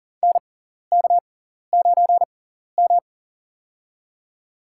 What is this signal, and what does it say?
Morse code, heard as a keyed tone, about 40 wpm, spaced as for 14 wpm, 700 Hz. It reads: NK9M
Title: USA Call Signs - 4 Characters - 6x Character Spacing - 40wpm